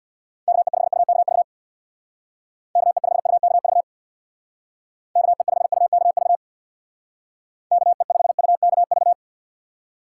Morse code, heard as a keyed tone, about 66 wpm, 700 Hz. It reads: KE5UKV KE5UKV KE5UKV KE5UKV